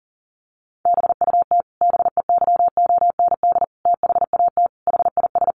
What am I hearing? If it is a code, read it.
Morse code, heard as a keyed tone, about 40 wpm, 700 Hz. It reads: BUT BEYOND THAT HIS
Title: Sentences from Top 1300 Words 40wpm